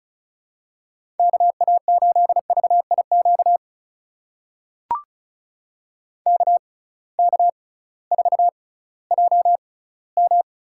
Morse code, heard as a keyed tone, about 34 wpm, spaced as for 19 wpm, 700 Hz. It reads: KA8VIQ KK4JM
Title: USA Call Signs - 6 Characters - 6x Character Spacing - 35wpm